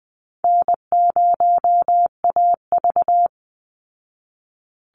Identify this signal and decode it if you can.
Morse code, heard as a keyed tone, about 20 wpm, 700 Hz. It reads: N0AV
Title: Easy Call Signs Repeated 2x - 20wpm